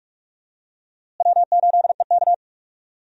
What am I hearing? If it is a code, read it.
Morse code, heard as a keyed tone, about 46 wpm, 700 Hz. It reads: W8EK
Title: USA Call Signs - 4 Characters - Repeated 2x - 45wpm